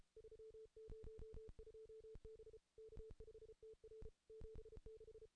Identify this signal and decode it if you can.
Morse code, heard as a keyed tone, about 33 wpm, 440 Hz. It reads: 302 B K5TR 76
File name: contests / ss-cw-k5tr-op-n5rz -2300